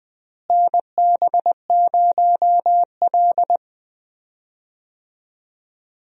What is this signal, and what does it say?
Morse code, heard as a keyed tone, about 20 wpm, 700 Hz. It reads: NB0L
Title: USA Call Signs - 4 Characters - Repeated 2x - 20wpm